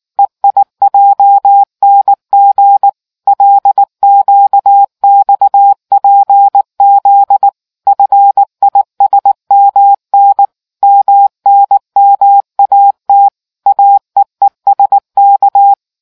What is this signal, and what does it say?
Morse code, heard as a keyed tone, about 19 wpm, 800 Hz. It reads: EIJNG LQXPZ FISMN MNMAT AEESK